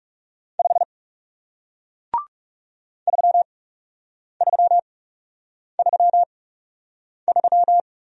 Morse code, heard as a keyed tone, about 40 wpm, spaced as for 8 wpm, 700 Hz. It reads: 5 3333